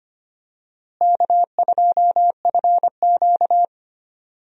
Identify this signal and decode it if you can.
Morse code, heard as a keyed tone, about 25 wpm, 700 Hz. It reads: K2FQ
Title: USA Call Signs - 4 Characters - Repeated 3x - 25wpm